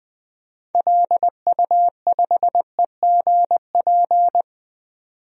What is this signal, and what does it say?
Morse code, heard as a keyed tone, about 20 wpm, 700 Hz. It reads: LU5EGP
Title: DXCC-100 Call Signs - 6 Characters - 20wpm